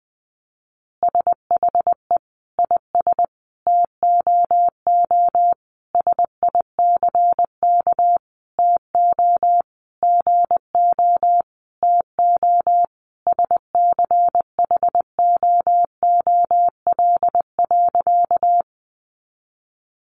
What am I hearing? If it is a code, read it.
Morse code, heard as a keyed tone, about 20 wpm, 700 Hz. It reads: SHE IS TOO SICK TO GO TO SCHOOL.
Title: Easy Phrases 20wpm